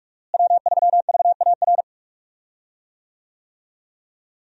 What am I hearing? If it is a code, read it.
Morse code, heard as a keyed tone, about 46 wpm, 700 Hz. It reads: W3VAR
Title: USA Call Signs - 5 Characters - Repeated 3x - 45wpm